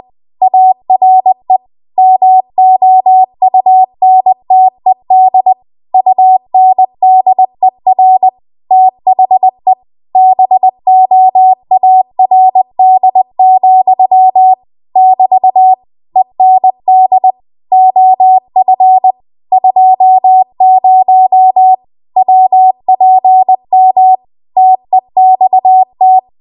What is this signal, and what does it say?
Morse code, heard as a keyed tone, about 20 wpm, 750 Hz. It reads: ARE MOUNTED UNDER THE BOARD, <BT> END OF 20 WPM TEXT